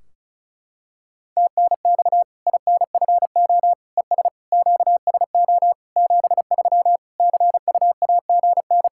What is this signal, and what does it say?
Morse code, heard as a keyed tone, about 34 wpm, 700 Hz. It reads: TNX INFO ES QSO 73 CUAGN
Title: Practice QSO - Exchange 4 - Ending - 35wpm